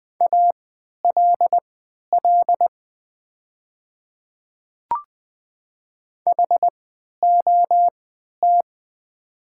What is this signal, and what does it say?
Morse code, heard as a keyed tone, about 20 wpm, spaced as for 11 wpm, 700 Hz. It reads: ALL HOT
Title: 3x Character Spacing - Top 200 Words - 20wpm